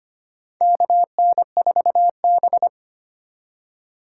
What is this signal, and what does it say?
Morse code, heard as a keyed tone, about 25 wpm, 700 Hz. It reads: KN4B
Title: USA Call Signs - 3x Character Spacing - 25wpm